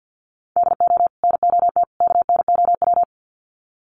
Morse code, heard as a keyed tone, about 50 wpm, 700 Hz. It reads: DO NOT KNOW
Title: Sets of 3 Words 50wpm